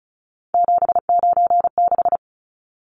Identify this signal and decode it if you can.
Morse code, heard as a keyed tone, about 34 wpm, 700 Hz. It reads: 796
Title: Numbers - 3-Digit 35wpm